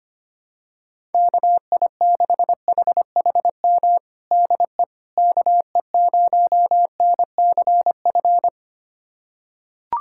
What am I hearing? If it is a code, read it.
Morse code, heard as a keyed tone, about 25 wpm, 700 Hz. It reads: KI6HHM DE KE0NCF
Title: USA Callsign DE Callsign 25wpm